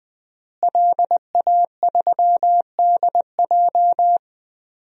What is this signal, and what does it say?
Morse code, heard as a keyed tone, about 21 wpm, 700 Hz. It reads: LA3DJ
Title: DXCC-100 Call Signs - 5 Characters - Repeated 2x - 20wpm